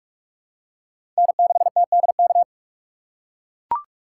Morse code, heard as a keyed tone, about 41 wpm, 700 Hz. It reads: N6TDK
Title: USA Call Signs - 5 Characters - 8x Character Spacing - 45wpm